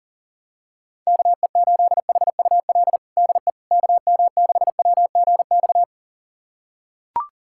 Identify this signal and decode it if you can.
Morse code, heard as a keyed tone, about 37 wpm, 700 Hz. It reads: KE8HUL DE KM6WGX
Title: USA Callsign DE Callsign 40wpm